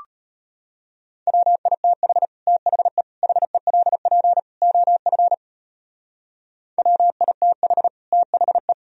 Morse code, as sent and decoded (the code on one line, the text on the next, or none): .-- .. - .... / - .... / .... .-.. .--. / --- ..-. / .-- .. - .... / - .... .
WITH THE HELP OF WITH THE